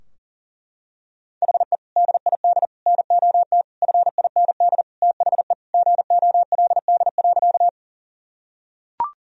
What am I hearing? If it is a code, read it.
Morse code, heard as a keyed tone, about 40 wpm, 700 Hz. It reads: HE DID NOT FIND THE GOLD.
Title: Sentences from Top 1000 Words - Review Part 3 - 40wpm